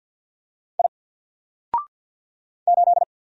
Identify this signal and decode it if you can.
Morse code, heard as a keyed tone, about 48 wpm, 700 Hz. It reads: I 8
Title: Single Letter-Number Alt 50wpm